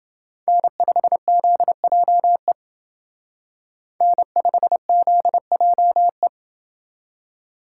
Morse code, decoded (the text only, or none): N5ZJE N5ZJE